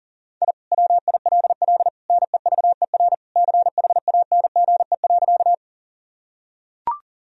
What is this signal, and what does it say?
Morse code, heard as a keyed tone, about 40 wpm, 700 Hz. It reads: I WILL NEVER CHANGE.
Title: Sentences from Top 1000 Words - Review Part 1 - 40wpm